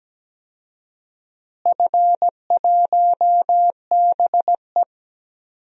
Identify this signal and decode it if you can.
Morse code, heard as a keyed tone, about 17 wpm, 700 Hz. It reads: F1BE
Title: DXCC-100 Call Signs - 4 Characters - Repeated 2x - 17wpm